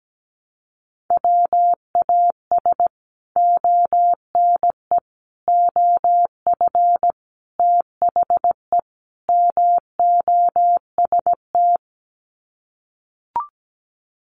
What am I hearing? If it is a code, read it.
Morse code, heard as a keyed tone, about 17 wpm, 700 Hz. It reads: WAS ONE OF THE MOST
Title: Sets of 5 Words 17wpm